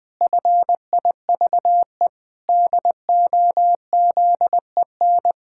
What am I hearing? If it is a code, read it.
Morse code, heard as a keyed tone, about 20 wpm, 700 Hz. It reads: FIVE DOZEN
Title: Pangram Challenge v2 - 20wpm